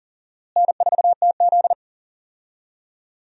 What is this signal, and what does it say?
Morse code, heard as a keyed tone, about 40 wpm, 700 Hz. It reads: N4TZ
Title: USA Call Signs - 4 Characters - Repeated 2x - 40wpm